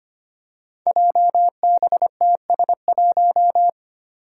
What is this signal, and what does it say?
Morse code, heard as a keyed tone, about 25 wpm, 700 Hz. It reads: JBTS1
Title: Five Challenging Letter-Number Combinations 25wpm